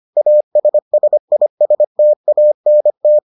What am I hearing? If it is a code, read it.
Morse code, heard as a keyed tone, about 25 wpm, 600 Hz. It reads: ASSISTANT